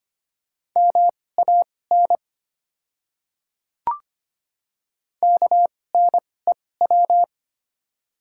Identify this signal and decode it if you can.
Morse code, heard as a keyed tone, about 25 wpm, spaced as for 18 wpm, 690 Hz. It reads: MAN KNEW
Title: 2x Character Spacing - Top 500 Words - 25wpm